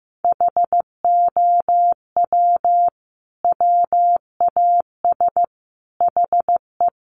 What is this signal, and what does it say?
Morse code, heard as a keyed tone, about 15 wpm, 700 Hz. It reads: HOW WAS HE
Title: Sentences from Top 1400 Words 15wpm